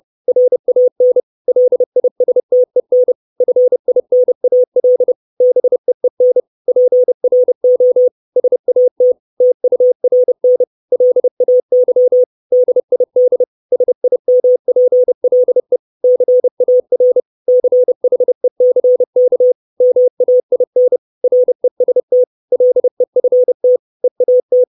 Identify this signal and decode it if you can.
Morse code, heard as a keyed tone, about 30 wpm, 500 Hz. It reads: RAN LISTEN FINAL BEEN PRO SAT TURN LAY DID SIMPLE CAR CHECK MAIN REST LEFT EAT
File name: 632 Words 30WPM